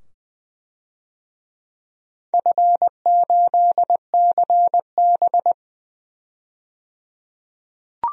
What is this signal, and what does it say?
Morse code, heard as a keyed tone, about 20 wpm, 700 Hz. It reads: F8CB
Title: DXCC-100 Call Signs - 4 Characters - 20wpm